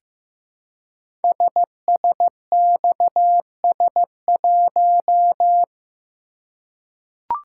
Morse code, heard as a keyed tone, about 15 wpm, 700 Hz. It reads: SSXS1
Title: Code Groups - 5 Letter:Numbers - 15wpm